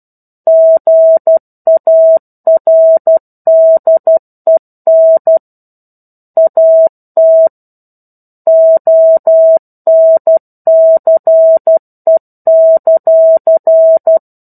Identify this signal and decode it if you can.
Morse code, heard as a keyed tone, about 12 wpm, 650 Hz. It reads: GARDEN AT ONCE;
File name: 01-Chapter-01 12 WPM